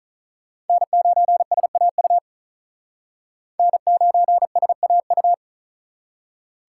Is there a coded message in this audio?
Yes: Morse code from 0.7 to 5.3 s, heard as a keyed tone, about 44 wpm, 700 Hz. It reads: N9SAU N9SAU